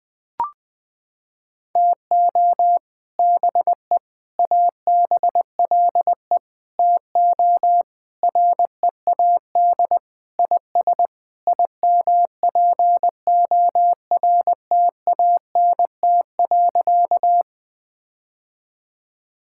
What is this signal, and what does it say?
Morse code, heard as a keyed tone, about 21 wpm, 700 Hz. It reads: TO BE ABLE TO READ IS IMPORTANT.
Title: Sentences with 3-Word Phrases - 20wpm